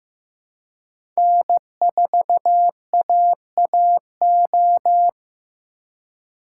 Morse code, heard as a keyed tone, about 15 wpm, 700 Hz. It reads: N4AAO
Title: USA Call Signs - 5 Characters - 2x Character Spacing - 15wpm